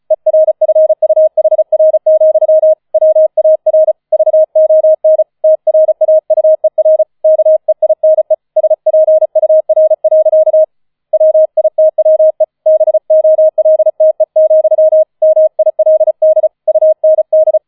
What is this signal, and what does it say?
Morse code, heard as a keyed tone, about 36 wpm, 630 Hz. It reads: ERFUHR, WAR VON TRAUER KEINE SPUR. WITWE BOLTE, MILD UND